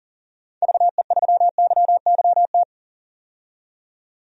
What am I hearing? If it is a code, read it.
Morse code, heard as a keyed tone, about 40 wpm, 700 Hz. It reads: VE3YYT
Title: DXCC-100 Call Signs - 6 Characters - 40wpm